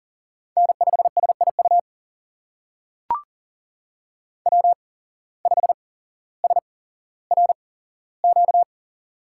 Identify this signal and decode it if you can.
Morse code, heard as a keyed tone, about 42 wpm, spaced as for 17 wpm, 700 Hz. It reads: N5SIU W5SRQ
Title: USA Call Signs - 8x Character Spacing - 40wpm